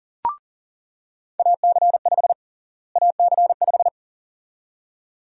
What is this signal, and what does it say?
Morse code, heard as a keyed tone, about 40 wpm, 700 Hz. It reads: AC5 AC5
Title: USA Call Sign Prefixes - Repeated 2x - 40wpm